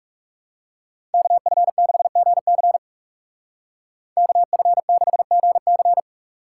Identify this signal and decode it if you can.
Morse code, heard as a keyed tone, about 39 wpm, 700 Hz. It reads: KF6GC KF6GC